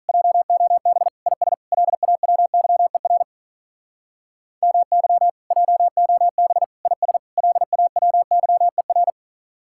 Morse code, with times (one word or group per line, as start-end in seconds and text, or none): 0.1-1.1 JOB
1.3-1.5 IS
1.7-3.2 LAWYER
4.6-5.3 MY
5.5-6.6 JOB
6.9-7.2 IS
7.4-9.1 LAWYER